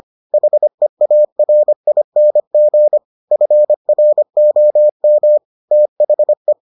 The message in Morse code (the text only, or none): HEARING FROM THE